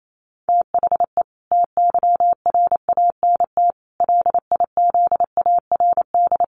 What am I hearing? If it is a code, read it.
Morse code, heard as a keyed tone, about 28 wpm, 700 Hz. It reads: THE TYRANT LIZARD
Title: A Sound Of Thunder 28wpm